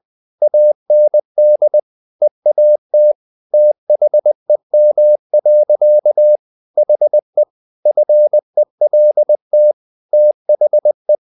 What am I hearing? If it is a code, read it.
Morse code, heard as a keyed tone, about 20 wpm, 600 Hz. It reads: AND EAT THEM. HE FELT THE
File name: the old man and the sea / the old man and the sea part 1 20wpm